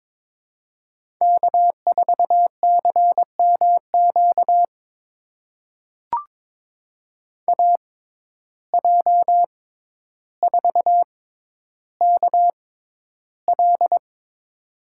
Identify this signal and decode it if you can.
Morse code, heard as a keyed tone, about 21 wpm, spaced as for 11 wpm, 700 Hz. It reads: K4CMQ AJ4KL
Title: USA Call Signs - 5 Characters - 6x Character Spacing - 22wpm